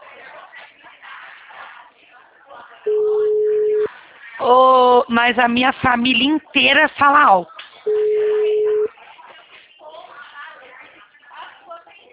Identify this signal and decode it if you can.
Ringback tone (Europe, 425 Hz), with speech.